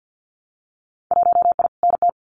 Morse code, heard as a keyed tone, about 50 wpm, 700 Hz. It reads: 1I NT
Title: Winter Field Day Exchanges Encore 50wpm